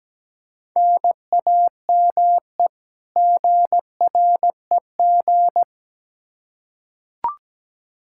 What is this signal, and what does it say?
Morse code, heard as a keyed tone, about 17 wpm, 700 Hz. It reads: NAME GREG
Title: Practice QSO - Exchange 1 - Beginning - 17wpm